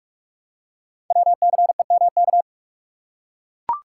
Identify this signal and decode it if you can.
Morse code, heard as a keyed tone, about 44 wpm, 690 Hz. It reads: WCEMK
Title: Code Groups - 5 Letters - 45wpm